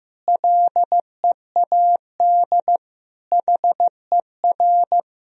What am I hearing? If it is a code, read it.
Morse code, heard as a keyed tone, about 15 wpm, 700 Hz. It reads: LEAD HER
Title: Sentences from Top 1000 Words - Review Part 2 - 15wpm